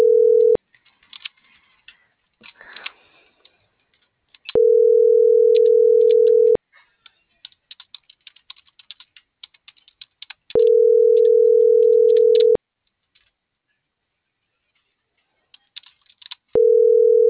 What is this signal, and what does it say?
Ringback tone (North America), 440 + 480 Hz.